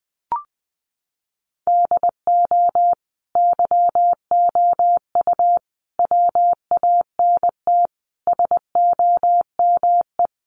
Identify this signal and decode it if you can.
Morse code, heard as a keyed tone, about 20 wpm, 700 Hz. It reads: DO YOU WANT SOME
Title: Sentences from Top 300 Words 20wpm